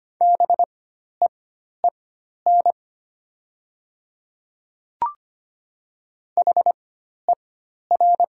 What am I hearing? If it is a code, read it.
Morse code, heard as a keyed tone, about 25 wpm, spaced as for 12 wpm, 690 Hz. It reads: BEEN HER
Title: 4x Character Spacing - Top 100 Words - 25wpm